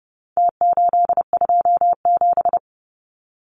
Morse code, heard as a keyed tone, about 30 wpm, 700 Hz. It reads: T827
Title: Numbers - Contest Style - 4-Digit 30wpm